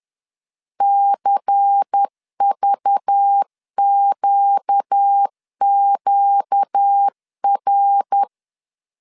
Morse code, heard as a keyed tone, about 10 wpm, 800 Hz. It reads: CVQQR